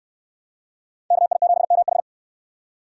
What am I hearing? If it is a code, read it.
Morse code, heard as a keyed tone, about 69 wpm, 700 Hz. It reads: KI7MH